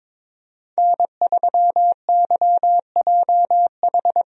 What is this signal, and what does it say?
Morse code, heard as a keyed tone, about 22 wpm, 700 Hz. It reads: N3YJH
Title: USA Call Signs - 5 Characters - Repeated 3x - 22wpm